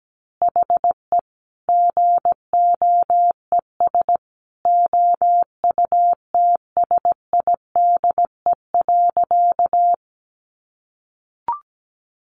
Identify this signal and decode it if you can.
Morse code, heard as a keyed tone, about 17 wpm, 700 Hz. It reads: HE GOES OUTSIDE.
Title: Additional Easy Phrases 17wpm